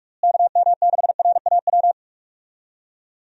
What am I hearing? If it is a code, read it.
Morse code, heard as a keyed tone, about 41 wpm, 700 Hz. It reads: KM6RAW
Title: USA Call Signs - 6 Characters - Repeated 3x - 45wpm